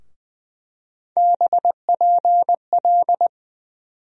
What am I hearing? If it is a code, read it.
Morse code, heard as a keyed tone, about 20 wpm, 700 Hz. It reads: BPL
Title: USA Call Sign Postfixes - 3 Letters 20wpm